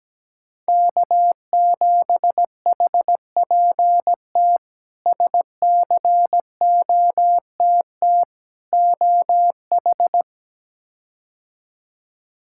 Morse code, heard as a keyed tone, about 17 wpm, 700 Hz. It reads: K7HPT SCOTT OH
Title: K1USN SST QSO Practice - 17wpm